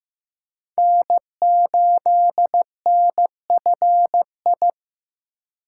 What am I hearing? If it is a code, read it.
Morse code, heard as a keyed tone, about 15 wpm, 700 Hz. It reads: N8NFI
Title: USA Call Signs - 5 Characters - 2x Character Spacing - 15wpm